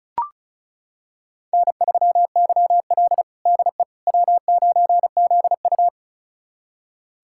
Morse code, heard as a keyed tone, about 36 wpm, 700 Hz. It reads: N3YL DE W9ZU
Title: USA Callsign DE Callsign - 4 Characters - 35wpm